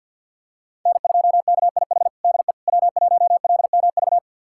Morse code, heard as a keyed tone, about 48 wpm, 690 Hz. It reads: N2KIH DE W1LMV
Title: USA Callsign DE Callsign - 5 Characters - 50wpm